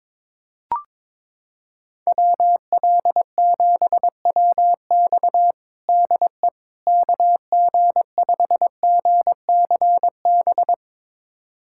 Morse code, heard as a keyed tone, about 22 wpm, 700 Hz. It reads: WL7WX DE KG5GCB
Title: USA Callsign DE Callsign 22wpm